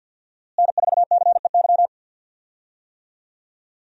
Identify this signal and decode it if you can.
Morse code, heard as a keyed tone, about 47 wpm, 700 Hz. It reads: N4CEY
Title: USA Call Signs - 5 Characters - Repeated 2x - 50wpm